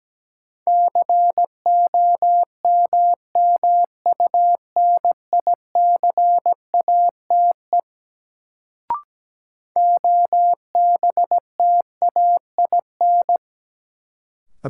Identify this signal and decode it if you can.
Morse code, heard as a keyed tone, about 17 wpm, 700 Hz. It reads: COMMUNICATE OBTAIN
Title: Sentences from Top 1300 Words - New Word Drill - 17wpm